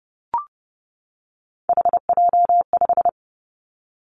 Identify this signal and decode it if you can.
Morse code, heard as a keyed tone, about 30 wpm, 700 Hz. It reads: HJ5
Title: Callsign Prefixes for Top 100 DXCC - 30wpm